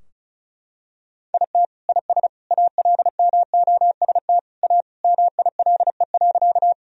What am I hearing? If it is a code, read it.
Morse code, heard as a keyed tone, about 36 wpm, 700 Hz. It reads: IT IS ALMOST A MILE.
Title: Sentences from Top 1000 Words - Review Part 4 - 35wpm